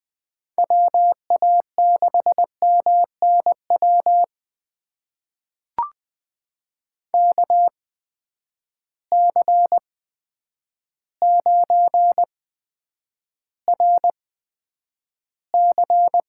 Morse code, heard as a keyed tone, about 20 wpm, spaced as for 9 wpm, 700 Hz. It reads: WA6MNW KC9RC
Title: USA Call Signs - 6 Characters - 8x Character Spacing - 20wpm